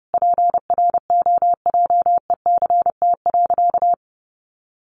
Morse code, heard as a keyed tone, about 30 wpm, 700 Hz. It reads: PROJECT.
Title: Sentences from Top 1400 Words 30wpm